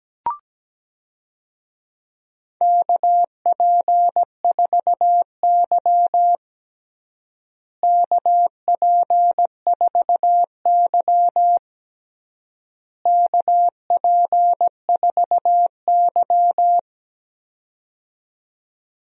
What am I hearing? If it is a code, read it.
Morse code, heard as a keyed tone, about 17 wpm, 700 Hz. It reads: KP4Y KP4Y KP4Y
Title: DXCC-100 Call Signs - 4 Characters - Repeated 3x - 17wpm